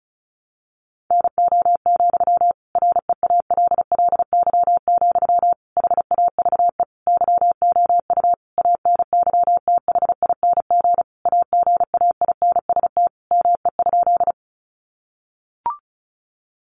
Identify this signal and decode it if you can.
Morse code, heard as a keyed tone, about 34 wpm, 700 Hz. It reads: NO, REALLY, HAVE YOU ANYTHING AGAINST ME?
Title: Sentences from Top 1000 Words 35wpm